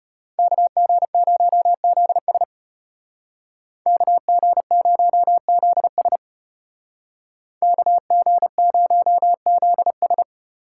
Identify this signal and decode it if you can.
Morse code, heard as a keyed tone, about 35 wpm, 700 Hz. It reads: KG0ZS KG0ZS KG0ZS